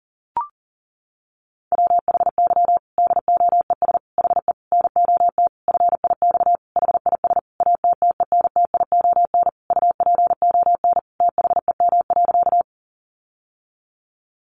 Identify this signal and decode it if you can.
Morse code, heard as a keyed tone, about 40 wpm, 700 Hz. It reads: WHY DOES HE NOT FIX HIS ATTENTION UPON THEM.
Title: Sentences from Top 1400 Words 40wpm